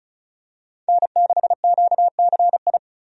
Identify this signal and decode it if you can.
Morse code, heard as a keyed tone, about 34 wpm, 700 Hz. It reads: N6QCI
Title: USA Call Signs - 5 Characters - 35wpm